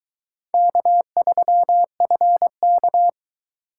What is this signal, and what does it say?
Morse code, heard as a keyed tone, about 22 wpm, 700 Hz. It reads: K3FK